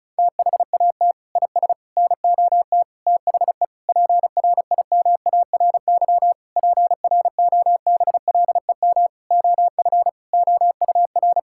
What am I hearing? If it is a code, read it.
Morse code, heard as a keyed tone, about 34 wpm, 700 Hz. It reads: THAT IS NOT THE PRIMARY PROBLEM OF OUR
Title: Sentences from Top 1000 Words - Review Part 5 - 35wpm